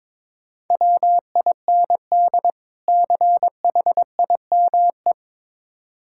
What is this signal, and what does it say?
Morse code, heard as a keyed tone, about 22 wpm, 690 Hz. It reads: WIND CHIME
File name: Difficult Words with Hints - 22wpm